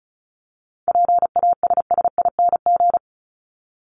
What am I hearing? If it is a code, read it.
Morse code, heard as a keyed tone, about 34 wpm, 700 Hz. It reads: PASSING
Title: ING Suffix - Top 100 Words with Base Words 35wpm